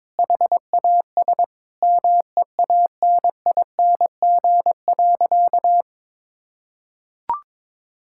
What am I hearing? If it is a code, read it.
Morse code, heard as a keyed tone, about 22 wpm, 700 Hz. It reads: HAS MEANING.
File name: practice / 22wpm / Sentences with 2-Word Phrases - 22wpm